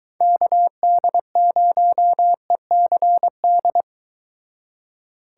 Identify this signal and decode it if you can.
Morse code, heard as a keyed tone, about 23 wpm, 700 Hz. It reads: KD0ECD